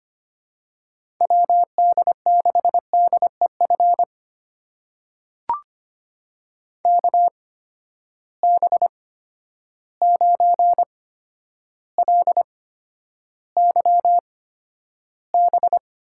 Morse code, heard as a keyed tone, about 25 wpm, spaced as for 11 wpm, 700 Hz. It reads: WD6DEF KB9LYB
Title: USA Call Signs - 6 Characters - 8x Character Spacing - 25wpm